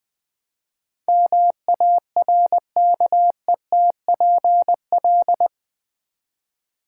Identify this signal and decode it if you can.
Morse code, heard as a keyed tone, about 20 wpm, 700 Hz. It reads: MARKETPL
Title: Build A Word - 4 Letters At A Time - 11-Letter Words - 20wpm